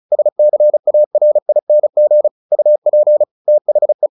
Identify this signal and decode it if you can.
Morse code, heard as a keyed tone, about 34 wpm, 600 Hz. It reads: SCARING UP THE
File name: the old man and the sea 2 35wpm